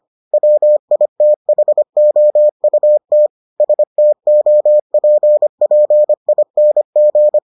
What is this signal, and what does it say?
Morse code, heard as a keyed tone, about 25 wpm, 600 Hz. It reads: WITHOUT STOPPING